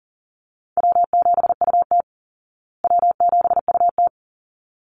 Morse code, heard as a keyed tone, about 40 wpm, 700 Hz. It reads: W7UT W7UT
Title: Easy Call Signs Repeated 2x - 40wpm